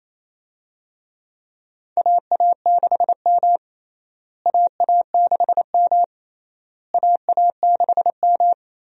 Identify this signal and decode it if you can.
Morse code, heard as a keyed tone, about 28 wpm, 700 Hz. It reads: AA6M AA6M AA6M
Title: USA Call Signs - 4 Characters - Repeated 3x - 28wpm